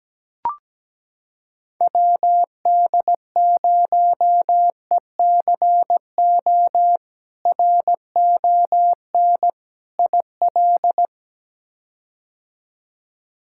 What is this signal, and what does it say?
Morse code, heard as a keyed tone, about 17 wpm, 700 Hz. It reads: WD0ECO RON IL
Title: K1USN SST QSO Practice - 17wpm